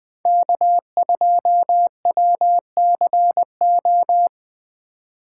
Morse code, heard as a keyed tone, about 20 wpm, 700 Hz. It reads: K2WCO